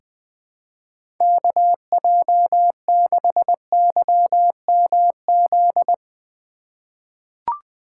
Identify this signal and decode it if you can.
Morse code, heard as a keyed tone, about 20 wpm, 700 Hz. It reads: KJ6YMZ
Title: USA Call Signs - 6 Characters - 2x Character Spacing - 20wpm